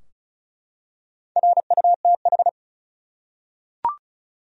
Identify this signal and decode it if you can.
Morse code, heard as a keyed tone, about 36 wpm, 700 Hz. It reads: RUTH
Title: USA Common Names - Max 4 Letters - 35wpm